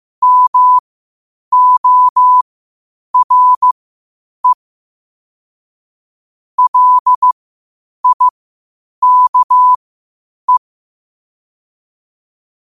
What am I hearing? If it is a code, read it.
Morse code, heard as a keyed tone, about 15 wpm, spaced as for 8 wpm, 1000 Hz. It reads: MORE LIKE